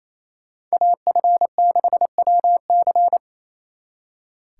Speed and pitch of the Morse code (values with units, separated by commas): 36 wpm, 700 Hz